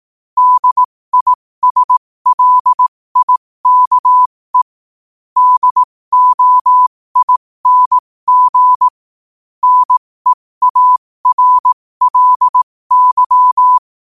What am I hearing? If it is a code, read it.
Morse code, heard as a keyed tone, about 18 wpm, spaced as for 15 wpm, 1000 Hz. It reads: DISLIKE DOING NEARLY
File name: qotdfiles / 15wpm